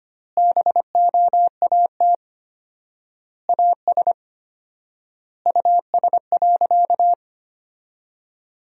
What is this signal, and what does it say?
Morse code, heard as a keyed tone, about 25 wpm, 700 Hz. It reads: BOAT AS US.
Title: 4x Word-Spacing - Sentences from Top 500 Words - 25wpm